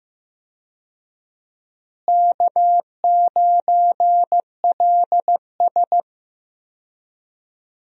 Morse code, heard as a keyed tone, about 15 wpm, 700 Hz. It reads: K9LS
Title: USA Call Signs - 4 Characters - Repeated 2x - 15wpm